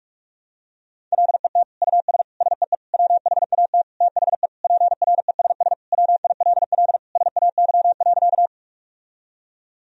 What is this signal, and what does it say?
Morse code, heard as a keyed tone, about 46 wpm, 700 Hz. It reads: LET US SEE WHAT THE PRESS WILL SAY.
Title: Sentences from Top 1000 Words - Review Part 1 - 45wpm